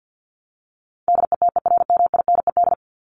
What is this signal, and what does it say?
Morse code, heard as a keyed tone, about 49 wpm, 700 Hz. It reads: DETERMINED
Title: ED Suffix - Top 100 Words with Base Words 50wpm